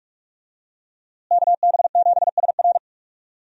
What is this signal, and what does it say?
Morse code, heard as a keyed tone, about 46 wpm, 700 Hz. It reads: KB7SR
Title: USA Call Signs - 5 Characters - Repeated 3x - 45wpm